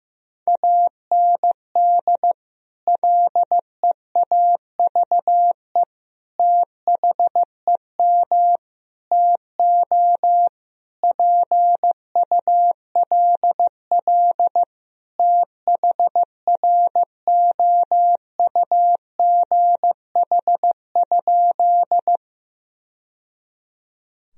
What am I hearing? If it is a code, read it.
Morse code, heard as a keyed tone, about 15 wpm, 700 Hz. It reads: AND LEAVE THEM TO PULL THROUGH?
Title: Sentences from Top 1000 Words - Review Part 2 - 15wpm